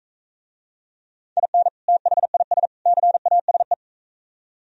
Morse code, heard as a keyed tone, about 43 wpm, 700 Hz. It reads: IN THIS CASE